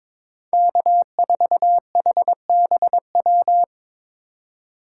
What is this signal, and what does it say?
Morse code, heard as a keyed tone, about 22 wpm, 700 Hz. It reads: K4HBW
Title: USA Call Signs - 5 Characters - 4x Character Spacing - 22wpm